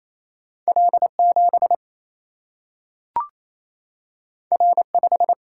Morse code, heard as a keyed tone, about 28 wpm, 700 Hz. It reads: L7 R5
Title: Single-Letter Single-Number 28wpm